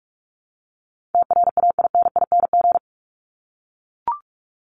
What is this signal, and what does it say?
Morse code, heard as a keyed tone, about 44 wpm, 700 Hz. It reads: TRAINING
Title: Top 501-750 Words 45wpm